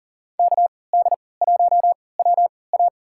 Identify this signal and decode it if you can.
Morse code, heard as a keyed tone, about 39 wpm, spaced as for 22 wpm, 700 Hz. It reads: KD1WA
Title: USA Call Signs - 5 Characters - 3x Character Spacing - 40wpm